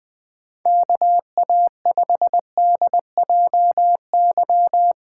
Morse code, heard as a keyed tone, about 20 wpm, 700 Hz. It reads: KA5DJY
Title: USA Call Signs - 6x Character Spacing - 20wpm